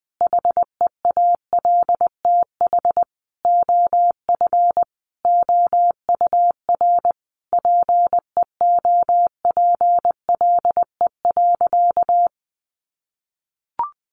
Morse code, heard as a keyed tone, about 20 wpm, 700 Hz. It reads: HEALTH OF OUR PEOPLE.
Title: Sentences from Top 1500 Words 20wpm